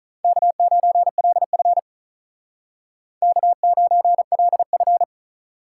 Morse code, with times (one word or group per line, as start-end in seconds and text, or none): 0.2-1.8 K9LF
3.2-5.0 K9LF